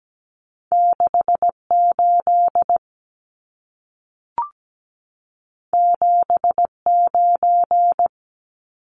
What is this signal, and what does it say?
Morse code, heard as a keyed tone, about 17 wpm, 700 Hz. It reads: 68 79
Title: Numbers - 2-Digit 17wpm